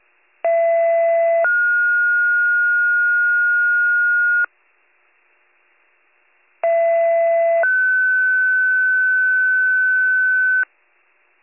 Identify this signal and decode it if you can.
Two-tone page, heard as a one-second tone, then a three-second tone, about 670 then 1430 Hz, about 670 then 1500 Hz. It is heard from 0.4 to 10.6 s.